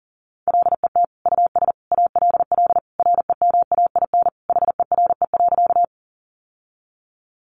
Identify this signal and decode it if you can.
Morse code, heard as a keyed tone, about 40 wpm, 700 Hz. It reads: LET US ALL REMAIN HERE.
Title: Sentences from Top 1200 Words 40wpm